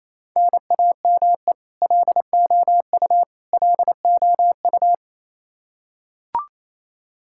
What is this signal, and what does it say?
Morse code, heard as a keyed tone, about 28 wpm, 700 Hz. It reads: NAME LOU LOU
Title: Practice QSO - Exchange 1 - Beginning - 28wpm